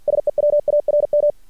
Morse code, heard as a keyed tone, about 50 wpm, 590 Hz. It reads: SE WARM
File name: archived / CRSnet